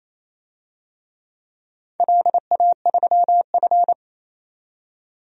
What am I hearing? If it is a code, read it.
Morse code, heard as a keyed tone, about 27 wpm, 690 Hz. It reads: LA3F
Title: DXCC-100 Call Signs - 4 Characters - 28wpm